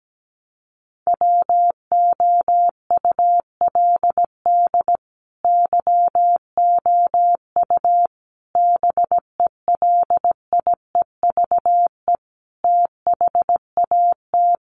Morse code, heard as a keyed tone, about 17 wpm, 700 Hz. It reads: WOULD YOU BELIEVE THAT
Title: Sentences from Top 1300 Words 17wpm